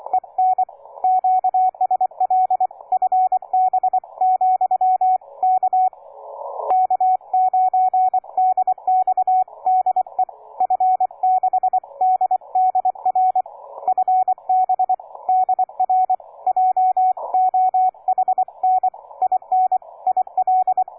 Morse code, heard as a keyed tone, about 26 wpm, 740 Hz, with radio noise. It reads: EN QSL FB , K K9DX DE F6DDR FB DR JOHN IN IL
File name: cw forum / K9DX 1818 le 17-12-2008 05h47